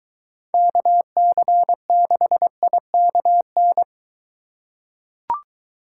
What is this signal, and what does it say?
Morse code, heard as a keyed tone, about 21 wpm, 700 Hz. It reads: KC6IKN